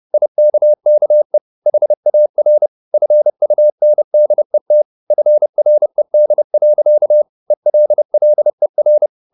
Morse code, heard as a keyed tone, about 30 wpm, 600 Hz. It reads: IKKE HAR FUNDET FRED. ELLER
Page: Ferie 30 wpm